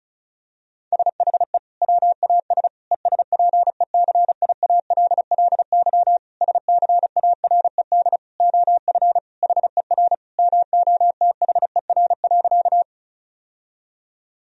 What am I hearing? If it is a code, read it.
Morse code, heard as a keyed tone, about 34 wpm, 700 Hz. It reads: SHE WAS ESPECIALLY SCARED OF HER MOTHER.
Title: Sentences from Top 1000 Words - Review Part 4 - 35wpm